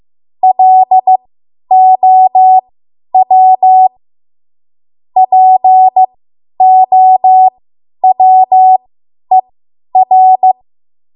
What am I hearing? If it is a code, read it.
Morse code, heard as a keyed tone, about 15 wpm, spaced as for 10 wpm, 750 Hz. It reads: LOW POWER
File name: Archive / 10 WPM / 10WPM